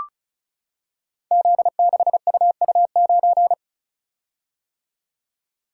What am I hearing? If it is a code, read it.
Morse code, heard as a keyed tone, about 36 wpm, 700 Hz. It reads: Z6UU9
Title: Code Groups - 5 Letter:Numbers - 35wpm